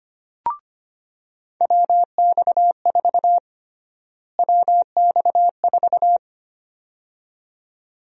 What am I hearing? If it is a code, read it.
Morse code, heard as a keyed tone, about 25 wpm, 700 Hz. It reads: WX4 WX4
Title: USA Call Sign Prefixes - Repeated 2x - 25wpm